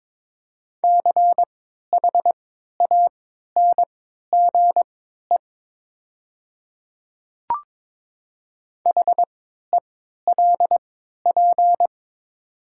Morse code, heard as a keyed tone, about 22 wpm, spaced as for 12 wpm, 700 Hz. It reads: CHANGE HELP
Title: 3x Character Spacing - Top 200 Words - 22wpm